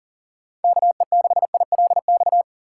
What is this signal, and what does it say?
Morse code, heard as a keyed tone, about 39 wpm, 700 Hz. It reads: KE6ILX
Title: USA Call Signs - 6 Characters - 4x Character Spacing - 40wpm